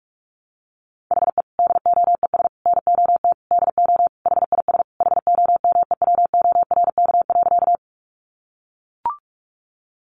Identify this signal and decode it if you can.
Morse code, heard as a keyed tone, about 44 wpm, 700 Hz. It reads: HE DOES NOT DO HIS HOMEWORK.
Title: Additional Easy Phrases 45wpm